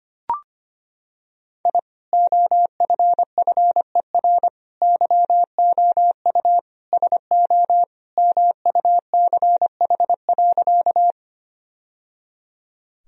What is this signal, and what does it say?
Morse code, heard as a keyed tone, about 25 wpm, 700 Hz. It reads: I OFFER YOU SO MUCH.
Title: Sentences from Top 1000 Words - Review Part 4 - 25wpm